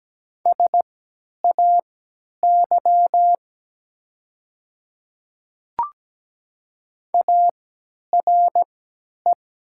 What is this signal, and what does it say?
Morse code, heard as a keyed tone, about 17 wpm, spaced as for 10 wpm, 700 Hz. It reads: SAY ARE
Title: 3x Character Spacing - Top 500 Words - 17wpm